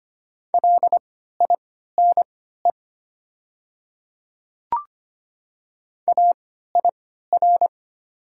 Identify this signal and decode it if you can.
Morse code, heard as a keyed tone, about 25 wpm, spaced as for 14 wpm, 700 Hz. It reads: LINE AIR
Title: 3x Character Spacing - Top 300 Words - 25wpm